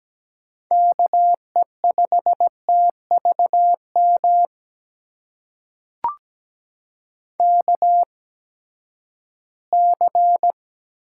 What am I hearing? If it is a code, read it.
Morse code, heard as a keyed tone, about 17 wpm, spaced as for 12 wpm, 700 Hz. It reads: KE5TVM KC
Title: USA Call Signs - 6 Characters - 8x Character Spacing - 17wpm